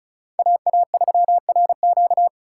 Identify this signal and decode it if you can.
Morse code, heard as a keyed tone, about 34 wpm, 700 Hz. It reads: AA3RQ
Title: USA Call Signs - 2x Character Spacing - 35wpm